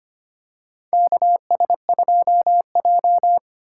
Morse code, heard as a keyed tone, about 25 wpm, 700 Hz. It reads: KS2J
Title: USA Call Signs - 4 Characters - 3x Character Spacing - 25wpm